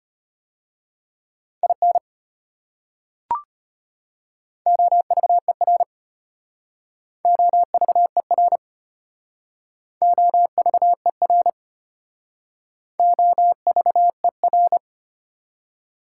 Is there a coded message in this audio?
Yes: Morse code from 1.6 to 14.8 s, heard as a keyed tone, about 33 wpm, 690 Hz. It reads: IN OVER OVER OVER OVER